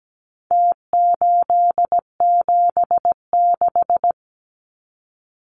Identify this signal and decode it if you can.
Morse code, heard as a keyed tone, about 17 wpm, 700 Hz. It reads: T876
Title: Numbers - Contest Style - 4-Digit 17wpm